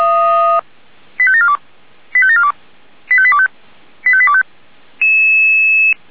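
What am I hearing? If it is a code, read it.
Five-tone selective call (ZVEI), heard as quick runs of five tones, tones of about 70 ms. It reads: 85742 85742 85825 85825
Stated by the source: ZVEI five-tone call received by SDR.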